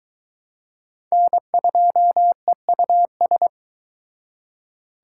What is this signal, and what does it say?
Morse code, heard as a keyed tone, about 23 wpm, 700 Hz. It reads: N2EUS